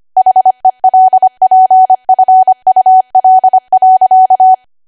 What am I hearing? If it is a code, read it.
Morse code, heard as a keyed tone, about 25 wpm, 750 Hz. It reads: HELPFUL.
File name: Archive / 25 WPM / 25WPM